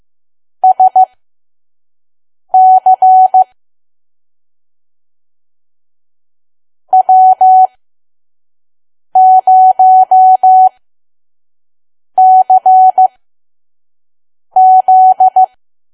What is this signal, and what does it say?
Morse code, heard as a keyed tone, about 15 wpm, spaced as for 5 wpm, 750 Hz. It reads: SC W0CZ
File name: Archive / 5 WPM files / PM